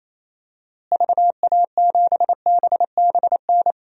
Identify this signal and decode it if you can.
Morse code, heard as a keyed tone, about 27 wpm, 700 Hz. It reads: VA7BBN
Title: DXCC-100 Call Signs - 6 Characters - Repeated 3x - 28wpm